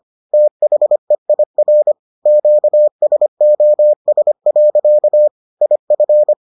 Morse code, heard as a keyed tone, about 25 wpm, 600 Hz. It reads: THEIR QSOS. IF